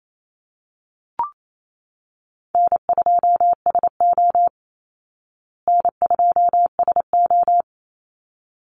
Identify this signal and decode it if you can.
Morse code, heard as a keyed tone, about 28 wpm, 700 Hz. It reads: N2SO N2SO
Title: Easy Call Signs Repeated 2x - 28wpm